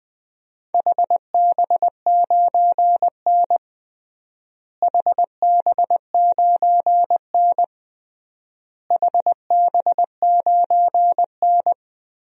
Morse code, heard as a keyed tone, about 20 wpm, 700 Hz. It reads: HB9N HB9N HB9N
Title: DXCC-100 Call Signs - 4 Characters - Repeated 3x - 20wpm